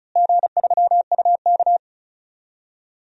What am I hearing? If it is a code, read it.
Morse code, heard as a keyed tone, about 34 wpm, 700 Hz. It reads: G3UK
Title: DXCC-100 Call Signs - 4 Characters - Repeated 3x - 35wpm